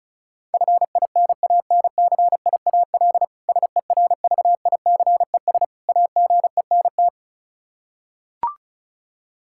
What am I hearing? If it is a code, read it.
Morse code, heard as a keyed tone, about 34 wpm, 700 Hz. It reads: FINANCIAL SERVICES AGENT
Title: K1USN SST QSO Practice 35wpm